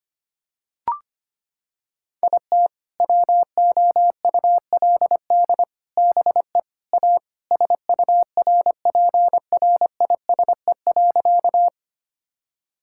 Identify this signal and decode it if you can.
Morse code, heard as a keyed tone, about 25 wpm, 700 Hz. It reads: IT WOULD BE A SURPRISE.
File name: practice / 25wpm / Sentences with 4-Word Phrases - 25wpm